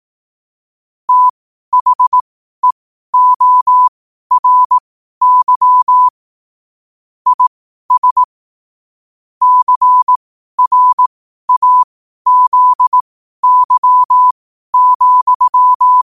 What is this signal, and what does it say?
Morse code, heard as a keyed tone, about 18 wpm, spaced as for 12 wpm, 1000 Hz. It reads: THEORY IS CRAZY,
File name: qotdfiles / 12wpm